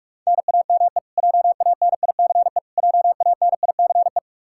Morse code, heard as a keyed tone, about 46 wpm, 700 Hz. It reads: NAME JANICE JANICE
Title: Practice QSO - Exchange 1 - Beginning - 45wpm